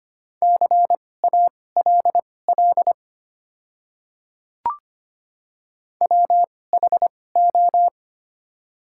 Morse code, heard as a keyed tone, about 25 wpm, spaced as for 18 wpm, 700 Hz. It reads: CALL WHO
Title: 2x Character Spacing - Top 500 Words - 25wpm